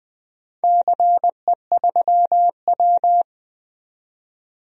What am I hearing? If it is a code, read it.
Morse code, heard as a keyed tone, about 20 wpm, 700 Hz. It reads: CE3W
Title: DXCC-100 Call Signs - 4 Characters - 20wpm